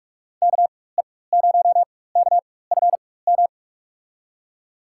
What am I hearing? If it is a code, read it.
Morse code, heard as a keyed tone, about 41 wpm, spaced as for 20 wpm, 700 Hz. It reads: KE0KFM
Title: USA Call Signs - 6 Characters - 4x Character Spacing - 45wpm